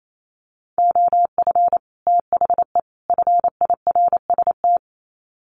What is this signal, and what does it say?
Morse code, heard as a keyed tone, about 28 wpm, 700 Hz. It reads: OF THE FIRST
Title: Sets of 3 Words 28wpm